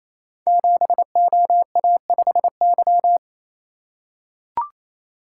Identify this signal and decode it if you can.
Morse code, heard as a keyed tone, about 27 wpm, 700 Hz. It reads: 7OA5Y
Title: Code Groups - 5 Letter:Numbers - 28wpm